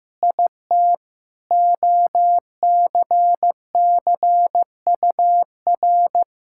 Morse code, heard as a keyed tone, about 15 wpm, 700 Hz. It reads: IT OCCUR
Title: Sentences from Top 1000 Words - Review Part 6 - 15wpm